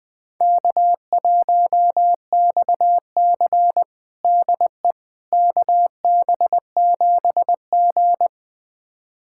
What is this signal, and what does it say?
Morse code, heard as a keyed tone, about 20 wpm, 700 Hz. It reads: K1XC DE KB7G
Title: USA Callsign DE Callsign - 4 Characters - 20wpm